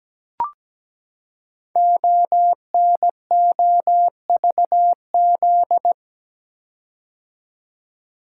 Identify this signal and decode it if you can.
Morse code, heard as a keyed tone, about 17 wpm, 700 Hz. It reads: ONOVZ
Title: Code Groups - 5 Letters - 17wpm